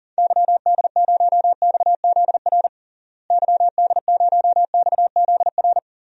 Morse code, heard as a keyed tone, about 40 wpm, 700 Hz. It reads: YD0XZR YD0XZR
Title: DXCC-100 Call Signs - 6 Characters - Repeated 2x - 40wpm